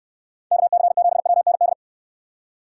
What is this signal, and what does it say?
Morse code, heard as a keyed tone, about 68 wpm, 700 Hz. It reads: KC7WND